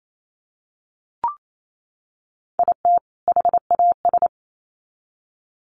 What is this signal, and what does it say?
Morse code, heard as a keyed tone, about 28 wpm, 700 Hz. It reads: IT HAS
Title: Sets of 2 Words 28wpm